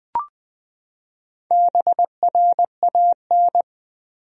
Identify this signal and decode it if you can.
Morse code, heard as a keyed tone, about 20 wpm, 700 Hz. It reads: BRAN
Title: USA Common Names - Max 4 Letters - 20wpm